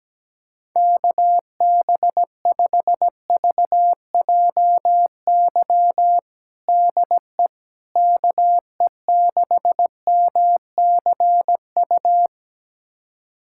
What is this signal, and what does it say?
Morse code, heard as a keyed tone, about 17 wpm, 700 Hz. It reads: KB5VJY DE KE6MCU
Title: USA Callsign DE Callsign 17wpm